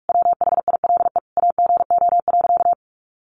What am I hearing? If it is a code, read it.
Morse code, heard as a keyed tone, about 44 wpm, 700 Hz. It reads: WHILE AGO.
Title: Sentences from Top 600 Words 45wpm